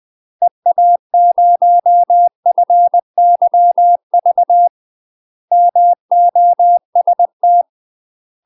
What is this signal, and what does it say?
Morse code, heard as a keyed tone, about 20 wpm, 700 Hz. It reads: EA0FYV MOST
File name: UK MASTER QSOs TRAINING / 20WPM / UK MASTER QSOs 0004